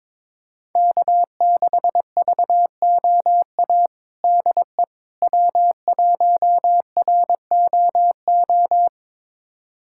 Morse code, heard as a keyed tone, about 22 wpm, 700 Hz. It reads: K6VOA DE W1ROO
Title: USA Callsign DE Callsign - 5 Characters - 22wpm